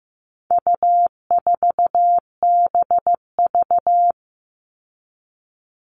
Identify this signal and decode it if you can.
Morse code, heard as a keyed tone, about 15 wpm, 700 Hz. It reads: U4BV
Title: All Cut Numbers - 4-Digit 15wpm